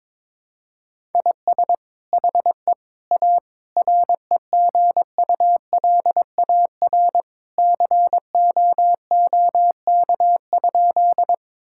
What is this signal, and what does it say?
Morse code, heard as a keyed tone, about 22 wpm, 700 Hz. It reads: IS HE A REGULAR COOK?
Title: Sentences from Top 1000 Words - Review Part 6 - 22wpm